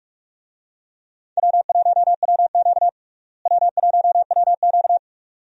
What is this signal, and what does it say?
Morse code, heard as a keyed tone, about 42 wpm, 700 Hz. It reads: W1WQ W1WQ
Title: USA Call Signs - 4 Characters - Repeated 2x - 45wpm